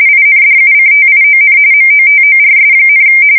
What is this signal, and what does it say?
RTTY, about 45.45 baud. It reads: INDUCTION MOTORS,